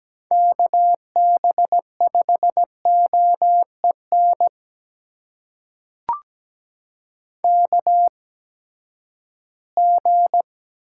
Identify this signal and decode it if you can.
Morse code, heard as a keyed tone, about 17 wpm, spaced as for 12 wpm, 700 Hz. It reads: KB5OEN KG
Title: USA Call Signs - 8x Character Spacing - 17wpm